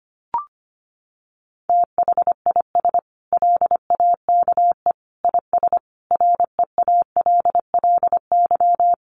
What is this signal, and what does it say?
Morse code, heard as a keyed tone, about 25 wpm, 700 Hz. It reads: THIS LAKE IS REALLY
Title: Sentences from Top 1100 Words 25wpm